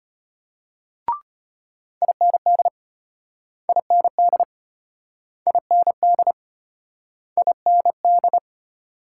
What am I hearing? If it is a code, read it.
Morse code, heard as a keyed tone, about 35 wpm, 700 Hz. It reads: IND IND IND IND